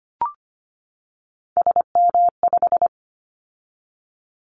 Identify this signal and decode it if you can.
Morse code, heard as a keyed tone, about 25 wpm, 700 Hz. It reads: SM5
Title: Callsign Prefixes for Top 25 DXCC - 25wpm